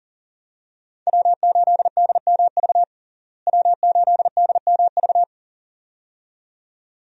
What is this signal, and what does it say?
Morse code, heard as a keyed tone, about 40 wpm, 700 Hz. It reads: W8DMV W8DMV
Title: USA Call Signs - 5 Characters - Repeated 2x - 40wpm